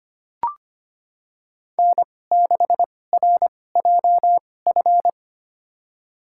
Morse code, heard as a keyed tone, about 25 wpm, spaced as for 18 wpm, 700 Hz. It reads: N6RJF
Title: USA Call Signs - 2x Character Spacing - 25wpm